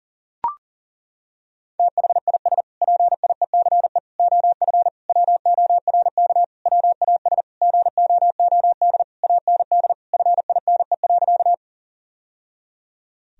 Morse code, heard as a keyed tone, about 40 wpm, 700 Hz. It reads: THIS PIECE OF WORK WAS GOOD AND FINE.
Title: Sentences from Top 1000 Words - Review Part 2 - 40wpm